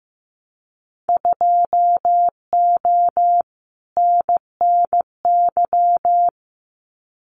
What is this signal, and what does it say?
Morse code, heard as a keyed tone, about 15 wpm, 700 Hz. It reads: 2O NNY
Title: Winter Field Day Exchanges 15wpm